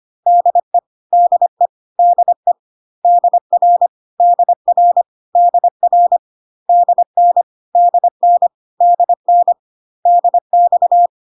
Morse code, heard as a keyed tone, about 25 wpm, 700 Hz. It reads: DE DE DE DR DR DR DN DN DN DX
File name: cw / CW common abbreviations alphabetical repeated 3 times 25 25wpm ik1hge